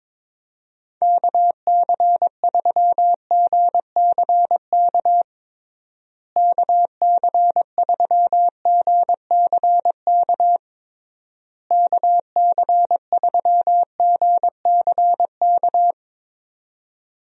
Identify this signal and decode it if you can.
Morse code, heard as a keyed tone, about 22 wpm, 700 Hz. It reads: KC3GCK KC3GCK KC3GCK
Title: USA Call Signs - 6 Characters - Repeated 3x - 22wpm